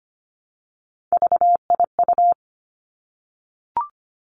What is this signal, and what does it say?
Morse code, heard as a keyed tone, about 25 wpm, 700 Hz. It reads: VIU
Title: Three Challenging Letter-Number Combinations 25wpm